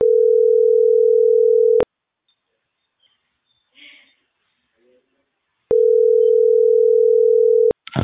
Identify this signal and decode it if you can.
Ringback tone (North America), 440 + 480 Hz.